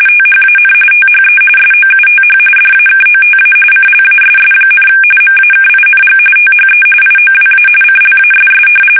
RTTY, about 45.45 baud. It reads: THAN THE SOUTH DAKOTA CLASS, MAKING THE IOWA CLASS
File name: fsk850 / battleshipnj